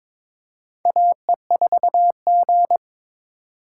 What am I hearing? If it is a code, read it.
Morse code, heard as a keyed tone, about 22 wpm, 700 Hz. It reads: AE4G
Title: USA Call Signs - 4x Character Spacing - 22wpm